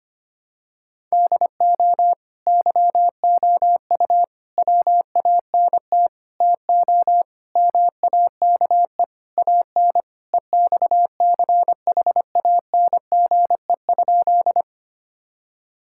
Morse code, heard as a keyed tone, about 25 wpm, 700 Hz. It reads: DO YOU WANT TO MAKE AN EXCHANGE?
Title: Sentences from Top 1000 Words - Review Part 6 - 25wpm